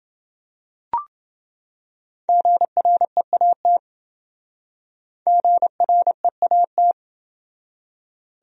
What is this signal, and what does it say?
Morse code, heard as a keyed tone, about 27 wpm, 700 Hz. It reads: GREAT GREAT